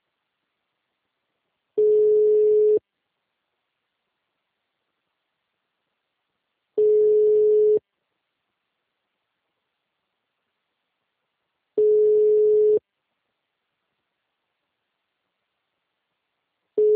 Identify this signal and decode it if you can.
Ringback tone (Europe, 425 Hz).